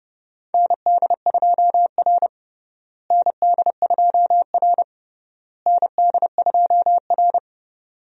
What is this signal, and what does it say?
Morse code, heard as a keyed tone, about 30 wpm, 700 Hz. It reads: ND2R ND2R ND2R
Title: USA Call Signs - 4 Characters - Repeated 3x - 30wpm